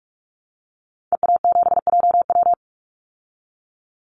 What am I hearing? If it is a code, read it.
Morse code, heard as a keyed tone, about 44 wpm, 700 Hz. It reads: EA7JW
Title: Call Signs Repeated 2x - 45wpm